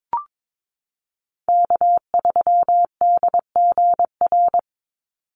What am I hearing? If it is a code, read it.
Morse code, heard as a keyed tone, about 22 wpm, 690 Hz. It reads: K3DGR
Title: Call Signs Repeated 2x - 22wpm